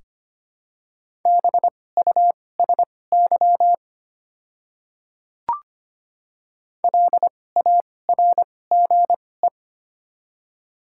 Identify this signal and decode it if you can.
Morse code, heard as a keyed tone, about 25 wpm, spaced as for 18 wpm, 700 Hz. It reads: BUSY LARGE
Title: 2x Character Spacing - Top 500 Words - 25wpm